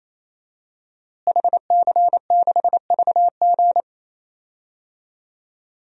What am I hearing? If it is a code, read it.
Morse code, heard as a keyed tone, about 28 wpm, 700 Hz. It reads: HC6VG
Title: DXCC-100 Call Signs - 5 Characters - Repeated 3x - 28wpm